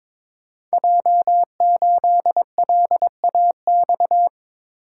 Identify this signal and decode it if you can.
Morse code, heard as a keyed tone, about 22 wpm, 690 Hz. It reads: J8LAX
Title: Code Groups - 5 Letter:Numbers - 22wpm